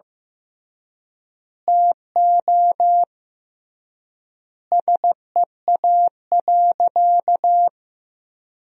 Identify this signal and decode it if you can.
Morse code, heard as a keyed tone, about 15 wpm, 700 Hz. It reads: TO SEA.
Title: 3x Word-Spacing - Sentences from Top 400 Words - 15wpm